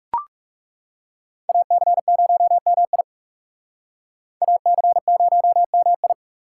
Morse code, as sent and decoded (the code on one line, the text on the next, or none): .- -.-. ----- -- .. / .- -.-. ----- -- ..
AC0MI AC0MI